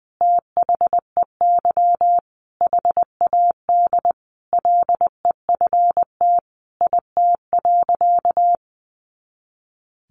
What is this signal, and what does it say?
Morse code, heard as a keyed tone, about 20 wpm, 700 Hz. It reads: THEY HAD LEFT IT.
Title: A Sound Of Thunder 20wpm